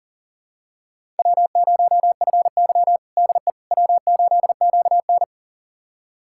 Morse code, heard as a keyed tone, about 40 wpm, 700 Hz. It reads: W0FY DE W8QN
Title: USA Callsign DE Callsign - 4 Characters - 40wpm